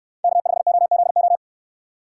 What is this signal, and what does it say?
Morse code, heard as a keyed tone, about 69 wpm, 700 Hz. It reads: K5OZO